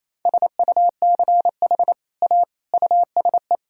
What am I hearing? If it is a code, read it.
Morse code, heard as a keyed tone, about 27 wpm, 700 Hz. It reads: SUCH A USE
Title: Sentences from Top 1000 Words - Review Part 3 - 28wpm